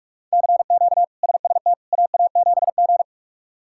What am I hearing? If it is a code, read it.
Morse code, heard as a keyed tone, about 42 wpm, 700 Hz. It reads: CQ SST AA7G
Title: K1USN SST QSO Practice - 45wpm